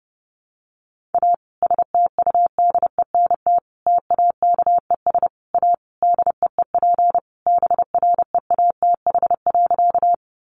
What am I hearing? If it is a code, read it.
Morse code, heard as a keyed tone, about 30 wpm, 700 Hz. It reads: A STUDENT TAKES A DEEP BREATH.
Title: Additional Easy Phrases 30wpm